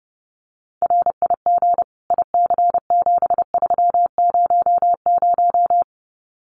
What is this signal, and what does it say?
Morse code, heard as a keyed tone, about 28 wpm, 700 Hz. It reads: RIG IC7300
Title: Common QSO Phrases Encore 30wpm